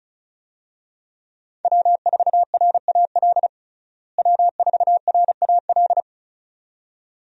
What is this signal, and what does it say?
Morse code, heard as a keyed tone, about 34 wpm, 700 Hz. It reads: W4RAL W4RAL
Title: USA Call Signs - 5 Characters - Repeated 2x - 35wpm